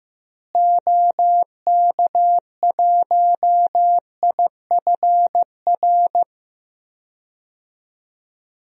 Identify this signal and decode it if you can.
Morse code, heard as a keyed tone, about 15 wpm, 700 Hz. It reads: OK1IFR
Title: DXCC-100 Call Signs - 6 Characters - Repeated 3x - 15wpm